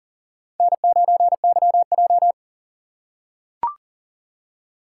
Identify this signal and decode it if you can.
Morse code, heard as a keyed tone, about 40 wpm, 700 Hz. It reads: N9YJ
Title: USA Call Signs - 6x Character Spacing - 40wpm